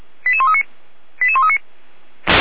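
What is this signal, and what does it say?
Five-tone selective call (ZVEI), heard as quick runs of five tones, tones of about 70 ms. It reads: 88139 88139
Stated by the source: ZVEI five-tone selective call 88335 received by SDR, from the Feuerwehr Orlamünde audio server.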